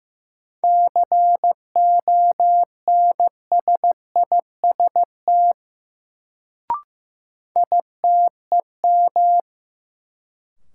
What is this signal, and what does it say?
Morse code, heard as a keyed tone, about 15 wpm, 700 Hz. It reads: CONSIST ITEM
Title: Sentences from Top 1400 Words - New Word Drill Alt - 15wpm